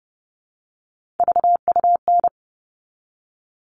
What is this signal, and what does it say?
Morse code, heard as a keyed tone, about 30 wpm, 700 Hz. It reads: VUN
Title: All Cut Numbers - 3-Digit 30wpm